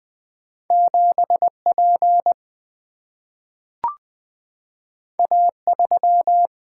Morse code, heard as a keyed tone, about 20 wpm, 700 Hz. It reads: 7 P A 3
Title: Single-Letter Single-Number 20wpm